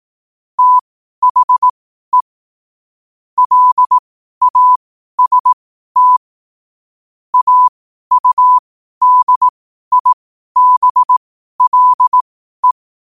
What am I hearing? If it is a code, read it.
Morse code, heard as a keyed tone, about 18 wpm, spaced as for 12 wpm, 1000 Hz. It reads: THE LAST AUDIBLE
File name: qotdfiles / 12wpm